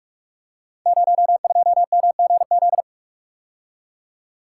Morse code, heard as a keyed tone, about 44 wpm, 700 Hz. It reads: 02MGZ
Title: Code Groups - 5 Letter:Numbers - 45wpm